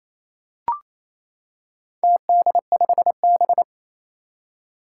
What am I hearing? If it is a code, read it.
Morse code, heard as a keyed tone, about 28 wpm, 700 Hz. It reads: TD5B
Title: Four Challenging Letter-Number Combinations 28wpm